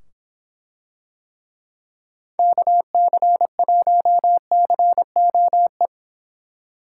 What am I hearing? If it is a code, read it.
Morse code, heard as a keyed tone, about 25 wpm, 700 Hz. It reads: KC1COE